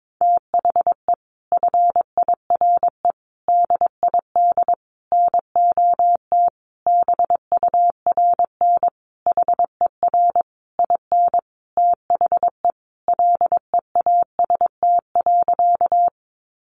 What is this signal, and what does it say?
Morse code, heard as a keyed tone, about 22 wpm, 700 Hz. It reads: THE FIRE DID NOT BURN HER IN THE LEAST.
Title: Sentences from Top 1300 Words 22wpm